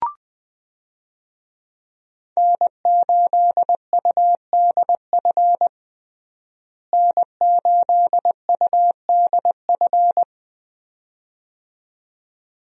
Morse code, heard as a keyed tone, about 20 wpm, 700 Hz. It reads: N8UDF N8UDF
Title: USA Call Signs - 5 Characters - Repeated 2x - 20wpm